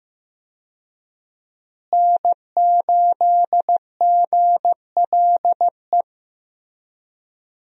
Morse code, heard as a keyed tone, about 15 wpm, 700 Hz. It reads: N8GLE
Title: USA Call Signs - 5 Characters - Repeated 2x - 15wpm